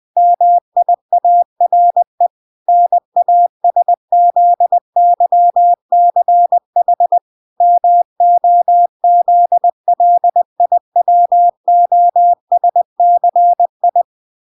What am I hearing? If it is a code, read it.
Morse code, heard as a keyed tone, about 20 wpm, 700 Hz. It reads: MIARE NASZYCH MOZLIWOSCI